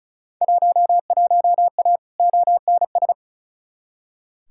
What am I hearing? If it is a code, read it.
Morse code, heard as a keyed tone, about 34 wpm, 700 Hz. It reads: 11A ONS
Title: Field Day Exchanges Encore - 35wpm